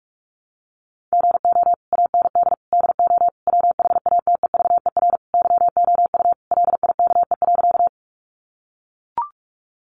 Morse code, heard as a keyed tone, about 44 wpm, 700 Hz. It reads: GO AND DO WHATEVER YOU LIKE.
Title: Sentences from Top 1100 Words 45wpm